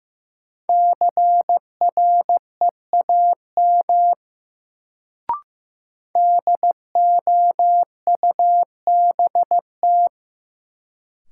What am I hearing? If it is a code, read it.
Morse code, heard as a keyed tone, about 15 wpm, 700 Hz. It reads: CREAM DOUBT
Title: Sentences from Top 1500 Words - New Word Drill - 15wpm